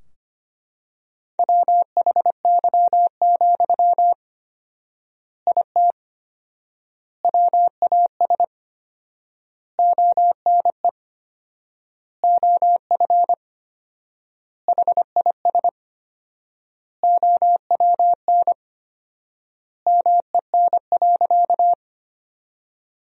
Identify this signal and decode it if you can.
Morse code, heard as a keyed tone, about 25 wpm, 700 Hz. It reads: WHY, IT WAS ONE OF HIS OWN MEN.
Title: 4x Word-Spacing - Sentences from Top 300 Words - 25wpm